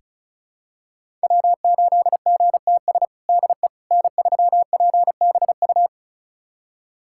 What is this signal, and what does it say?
Morse code, heard as a keyed tone, about 34 wpm, 700 Hz. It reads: W8GTS DE N3PBU
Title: USA Callsign DE Callsign - 5 Characters - 35wpm